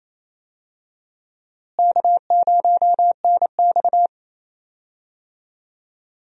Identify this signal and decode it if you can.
Morse code, heard as a keyed tone, about 28 wpm, 700 Hz. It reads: K0NX
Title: USA Call Signs - 4 Characters - Repeated 2x - 28wpm